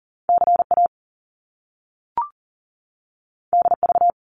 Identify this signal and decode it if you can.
Morse code, heard as a keyed tone, about 42 wpm, 700 Hz. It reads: CA DV
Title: Top Two Letter Combinations in Words 40wpm